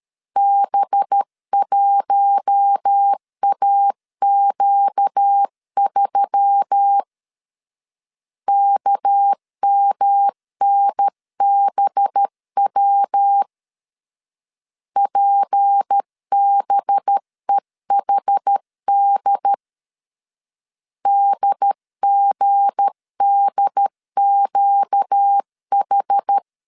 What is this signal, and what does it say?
Morse code, heard as a keyed tone, about 13 wpm, 800 Hz. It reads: B1AQ3 KMNBW PBEHD DGDQH
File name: LEZIONI / 7 GF9Z / 7 G 12wpm 2055